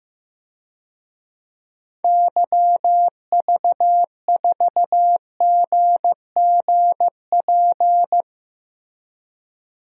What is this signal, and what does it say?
Morse code, heard as a keyed tone, about 15 wpm, 700 Hz. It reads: YV4GGP
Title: DXCC-100 Call Signs - 6 Characters - Repeated 2x - 15wpm